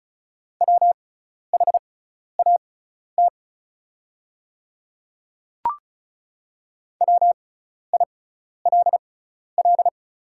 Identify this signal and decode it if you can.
Morse code, heard as a keyed tone, about 36 wpm, spaced as for 12 wpm, 700 Hz. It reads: WHAT WILL
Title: 6x Character Spacing - Top 200 Words - 35wpm